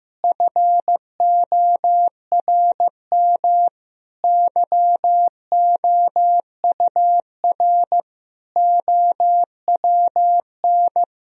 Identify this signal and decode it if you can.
Morse code, heard as a keyed tone, about 15 wpm, 700 Hz. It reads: FORM YOUR OWN
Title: Sentences from Top 1000 Words - Review Part 1 - 15wpm